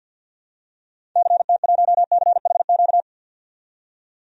Morse code, heard as a keyed tone, about 51 wpm, 700 Hz. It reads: CT1CHQ
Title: DXCC-100 Call Signs - 6 Characters - 50wpm